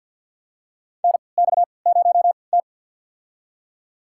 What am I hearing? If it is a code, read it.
Morse code, heard as a keyed tone, about 47 wpm, spaced as for 28 wpm, 700 Hz. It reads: NX0T
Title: USA Call Signs - 4 Characters - 3x Character Spacing - 50wpm